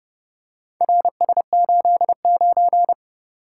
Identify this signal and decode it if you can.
Morse code, heard as a keyed tone, about 29 wpm, 700 Hz. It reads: RS89